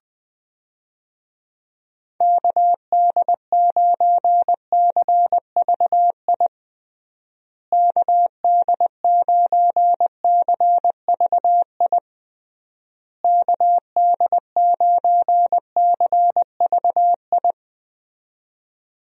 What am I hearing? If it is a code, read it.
Morse code, heard as a keyed tone, about 20 wpm, 700 Hz. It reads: KD9CVI KD9CVI KD9CVI
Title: USA Call Signs - 6 Characters - Repeated 3x - 20wpm